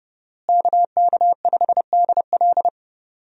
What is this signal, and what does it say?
Morse code, heard as a keyed tone, about 31 wpm, 700 Hz. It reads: KK5DL
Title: USA Call Signs - 5 Characters - Repeated 3x - 30wpm